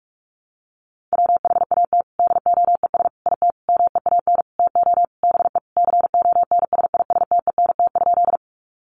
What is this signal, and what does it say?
Morse code, heard as a keyed tone, about 44 wpm, 700 Hz. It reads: WHAT DOES IT MEAN TO BE CONSISTENT?
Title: Sentences from Top 1500 Words 45wpm